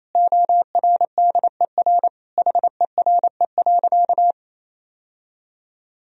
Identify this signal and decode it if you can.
Morse code, heard as a keyed tone, about 28 wpm, 700 Hz. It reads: ORDER HERE.
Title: Sentences from Top 1000 Words - Review Part 2 - 28wpm